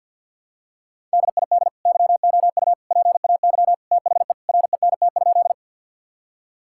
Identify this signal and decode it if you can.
Morse code, heard as a keyed tone, about 46 wpm, 700 Hz. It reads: DID YOU PAY THE RENT?
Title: Sentences from Top 1000 Words - Review Part 5 - 50wpm